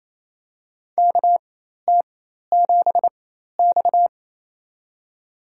Morse code, heard as a keyed tone, about 28 wpm, spaced as for 13 wpm, 700 Hz. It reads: KT7X
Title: USA Call Signs - 4 Characters - 4x Character Spacing - 28wpm